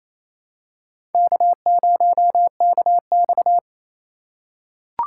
Morse code, heard as a keyed tone, about 28 wpm, 700 Hz. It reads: K0KX
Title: K1USN SST QSO Practice - 28wpm